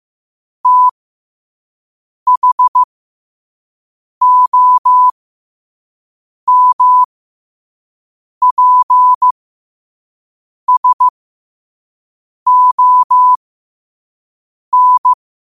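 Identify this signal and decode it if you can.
Morse code, heard as a keyed tone, about 15 wpm, spaced as for 5 wpm, 1000 Hz. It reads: THOMPSON
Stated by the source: 5 WPM Morse quote of the day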